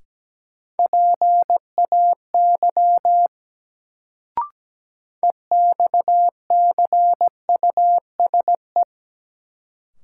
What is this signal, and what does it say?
Morse code, heard as a keyed tone, about 17 wpm, 700 Hz. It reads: PAY EXCUSE
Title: Sentences from Top 700 Words - New Word Drill - 17wpm